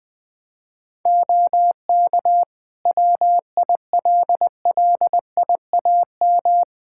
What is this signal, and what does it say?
Morse code, heard as a keyed tone, about 20 wpm, 700 Hz. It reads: OK WILLIAM
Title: Practice QSO - Exchange 2 - Info - 20wpm